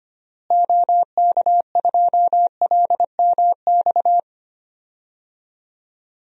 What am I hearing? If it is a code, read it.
Morse code, heard as a keyed tone, about 25 wpm, 700 Hz. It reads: OK2LMX
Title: DXCC-100 Call Signs - 6 Characters - Repeated 3x - 25wpm